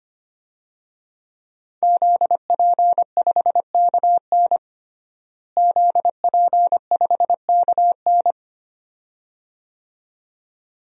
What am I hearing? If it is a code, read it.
Morse code, heard as a keyed tone, about 25 wpm, 700 Hz. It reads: ZP5KN ZP5KN
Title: DXCC-100 Call Signs - 5 Characters - Repeated 2x - 25wpm